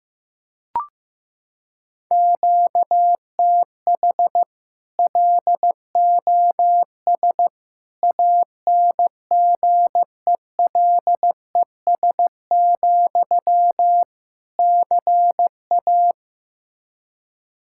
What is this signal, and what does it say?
Morse code, heard as a keyed tone, about 16 wpm, 700 Hz. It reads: QTH LOS ANGELES, CA
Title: Practice QSO - Exchange 1 - Beginning - 15wpm